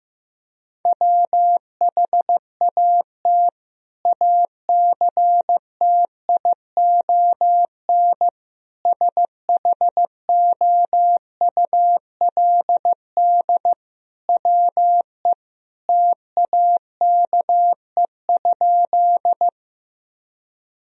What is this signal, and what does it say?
Morse code, heard as a keyed tone, about 15 wpm, 700 Hz. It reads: WHAT ACTION SHOULD WE TAKE?
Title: Sentences from Top 1000 Words - Review Part 4 - 15wpm